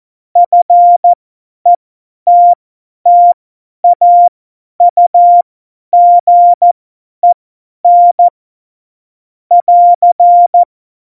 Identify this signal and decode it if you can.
Morse code, heard as a keyed tone, about 14 wpm, spaced as for 10 wpm, 700 Hz. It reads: FETTAUGEN <AR>